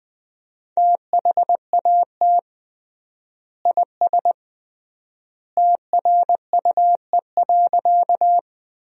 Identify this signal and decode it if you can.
Morse code, heard as a keyed tone, about 20 wpm, 700 Hz. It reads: THAT IS TRUE.
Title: 3x Word-Spacing - Sentences from Top 400 Words - 20wpm